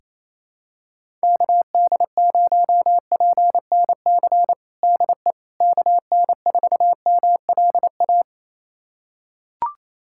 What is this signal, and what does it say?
Morse code, heard as a keyed tone, about 28 wpm, 700 Hz. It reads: KD0PNC DE KN4MLA
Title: USA Callsign DE Callsign - 6 Characters - 28wpm